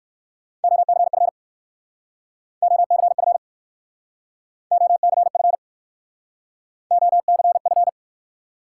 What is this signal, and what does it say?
Morse code, heard as a keyed tone, about 59 wpm, 700 Hz. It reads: OCF OCF OCF OCF